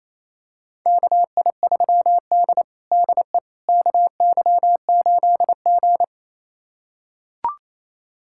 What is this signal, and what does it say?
Morse code, heard as a keyed tone, about 28 wpm, 700 Hz. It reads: KI3D DE KY8G
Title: USA Callsign DE Callsign - 4 Characters - 28wpm